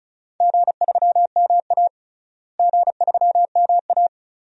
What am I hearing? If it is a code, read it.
Morse code, heard as a keyed tone, about 34 wpm, 700 Hz. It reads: G3MA G3MA
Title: DXCC-100 Call Signs - 4 Characters - Repeated 2x - 35wpm